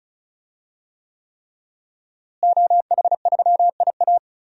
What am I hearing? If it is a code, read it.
Morse code, heard as a keyed tone, about 34 wpm, 700 Hz. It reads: OH3IA
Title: DXCC-100 Call Signs - 5 Characters - Repeated 3x - 35wpm